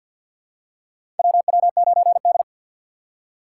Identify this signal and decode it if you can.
Morse code, heard as a keyed tone, about 52 wpm, 700 Hz. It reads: WW9D
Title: USA Call Signs - 3x Character Spacing - 50wpm